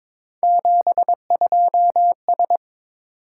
Morse code, heard as a keyed tone, about 22 wpm, 700 Hz. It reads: 72S
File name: practice / 22wpm / Three Letter-Number Combinations - 22wpm